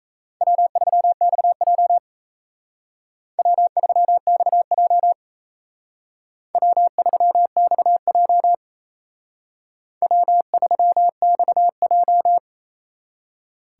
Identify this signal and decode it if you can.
Morse code, heard as a keyed tone, about 36 wpm, 700 Hz. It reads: W3XJ W3XJ W3XJ W3XJ